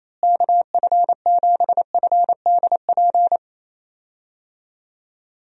Morse code, heard as a keyed tone, about 28 wpm, 700 Hz. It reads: KF7FDP
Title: USA Call Signs - 6 Characters - Repeated 3x - 28wpm